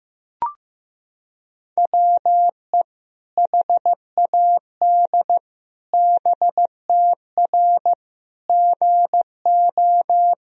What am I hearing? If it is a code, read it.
Morse code, heard as a keyed tone, about 15 wpm, 700 Hz. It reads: WE HAD BTR GO
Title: Sentences from Top 400 Words and Abbreviations 15wpm